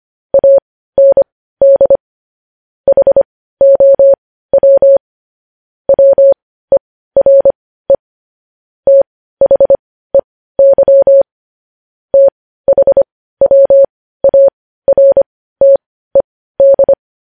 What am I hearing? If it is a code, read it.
Morse code, heard as a keyed tone, about 25 wpm, spaced as for 15 wpm, 550 Hz. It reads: AND HOW WERE THEY THWARTED